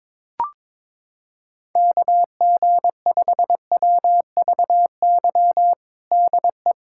Morse code, heard as a keyed tone, about 21 wpm, 700 Hz. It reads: KG5WVY DE
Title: USA Callsign DE Callsign - 6 Characters - 22wpm